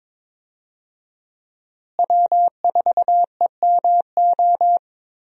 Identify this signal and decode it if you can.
Morse code, heard as a keyed tone, about 22 wpm, 700 Hz. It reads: W4EMO
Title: USA Call Signs - 5 Characters - Repeated 2x - 22wpm